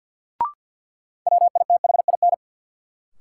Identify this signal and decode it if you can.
Morse code, heard as a keyed tone, about 52 wpm, 700 Hz. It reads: WITHIN
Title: Sets of 2 Words - New Word Drill - 50wpm